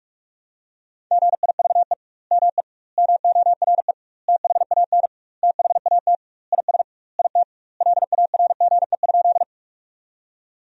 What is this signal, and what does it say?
Morse code, heard as a keyed tone, about 46 wpm, 700 Hz. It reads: GIVE ME MORE THAN THAT IS IT LARGE?
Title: 2x Word-Spacing - Sentences from Top 200 Words - 45wpm